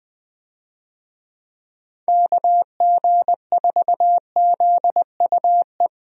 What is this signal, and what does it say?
Morse code, heard as a keyed tone, about 20 wpm, 700 Hz. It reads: KG4ZUE
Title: USA Call Signs - 6 Characters - Repeated 2x - 20wpm